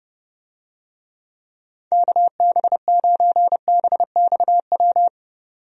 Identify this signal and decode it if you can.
Morse code, heard as a keyed tone, about 31 wpm, 700 Hz. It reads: KB9BXW
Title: USA Call Signs - 6 Characters - 30wpm